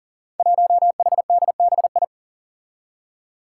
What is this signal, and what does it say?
Morse code, heard as a keyed tone, about 41 wpm, 700 Hz. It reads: 1HDBI
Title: Five Challenging Letter-Number Combinations 40wpm